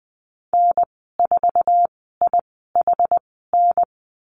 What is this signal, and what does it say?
Morse code, heard as a keyed tone, about 20 wpm, spaced as for 14 wpm, 700 Hz. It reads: N4IHN
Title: 076 Lesson 13g - Letter L - Call Signs 20wpm